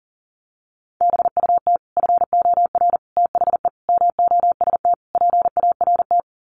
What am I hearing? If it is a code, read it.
Morse code, heard as a keyed tone, about 40 wpm, 700 Hz. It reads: BUT FOR THE MOST PART
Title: Sets of 5 Words Encore 40wpm